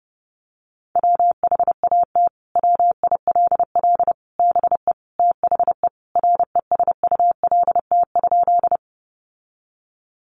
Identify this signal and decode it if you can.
Morse code, heard as a keyed tone, about 30 wpm, 700 Hz. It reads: WHAT WILL BE THE RESULT?
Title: Sentences from Top 700 Words 30wpm